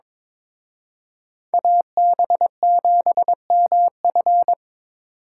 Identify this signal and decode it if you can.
Morse code, heard as a keyed tone, about 26 wpm, 700 Hz. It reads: AB7MF
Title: Speed Racing - USA 2x2 Call Signs 22wpm